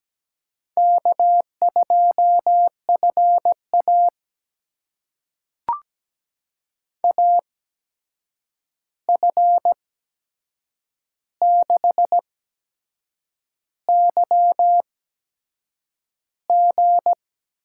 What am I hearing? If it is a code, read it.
Morse code, heard as a keyed tone, about 17 wpm, spaced as for 7 wpm, 700 Hz. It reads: K2FA AF6YG
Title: USA Call Signs - 8x Character Spacing - 17wpm